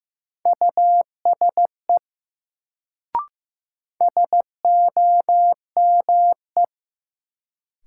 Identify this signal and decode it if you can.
Morse code, heard as a keyed tone, about 15 wpm, 700 Hz. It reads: USE SOME
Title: Sentences from Top 100 Words - New Word Drill - 15wpm